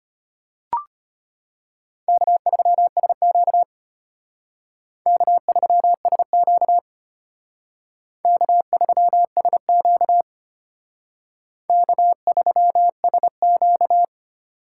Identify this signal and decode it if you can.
Morse code, heard as a keyed tone, about 32 wpm, 700 Hz. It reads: K3SQ K3SQ K3SQ K3SQ